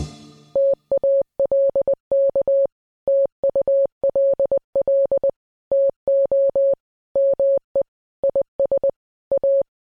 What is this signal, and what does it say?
Morse code, heard as a keyed tone, about 20 wpm, 550 Hz. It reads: TALK TULL TO ME IS A